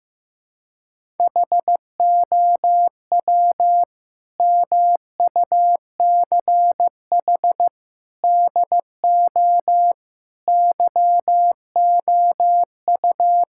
Morse code, heard as a keyed tone, about 15 wpm, 700 Hz. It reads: HOW MUCH DO YOU
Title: Sentences from Top 1000 Words - Review Part 6 - 15wpm